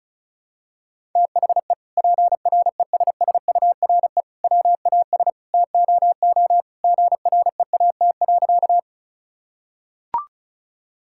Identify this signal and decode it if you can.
Morse code, heard as a keyed tone, about 34 wpm, 700 Hz. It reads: THE PRESSURE WAS TOO GREAT.
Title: Sentences from Top 1000 Words - Review Part 6 - 35wpm